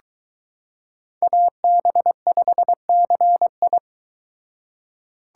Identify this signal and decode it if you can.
Morse code, heard as a keyed tone, about 23 wpm, 700 Hz. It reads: AB5CI